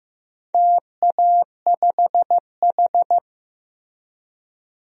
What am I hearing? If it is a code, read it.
Morse code, heard as a keyed tone, about 15 wpm, 700 Hz. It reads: TA5H
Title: DXCC-100 Call Signs - 4 Characters - Repeated 2x - 15wpm